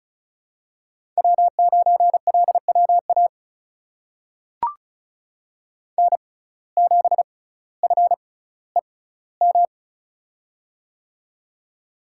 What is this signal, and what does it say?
Morse code, heard as a keyed tone, about 36 wpm, spaced as for 18 wpm, 700 Hz. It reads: W9LWA N7FEM
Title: USA Call Signs - 5 Characters - 6x Character Spacing - 35wpm